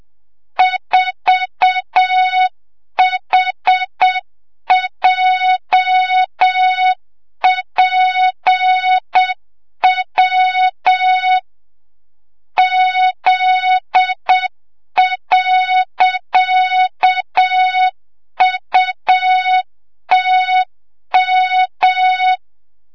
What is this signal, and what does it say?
Morse code, heard as a keyed tone, about 7 wpm, 3700 Hz, with noise. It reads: 4HJPW Z.UTM
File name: Morze-HA2MN / hang / 25 GYAKORLAT